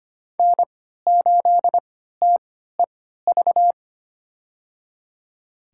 Morse code, heard as a keyed tone, about 25 wpm, spaced as for 14 wpm, 700 Hz. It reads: N8TEV
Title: USA Call Signs - 3x Character Spacing - 25wpm